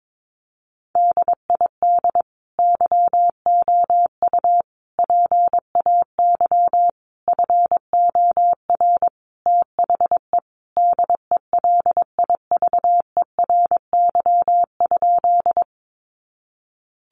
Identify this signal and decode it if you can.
Morse code, heard as a keyed tone, about 22 wpm, 700 Hz. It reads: DID YOU PAY FOR THE DELIVERY?
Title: Sentences from Top 1500 Words 22wpm